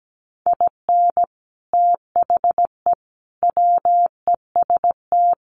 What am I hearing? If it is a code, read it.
Morse code, heard as a keyed tone, about 17 wpm, 700 Hz. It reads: IN THE WEST
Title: Sentences from Top 1300 Words 17wpm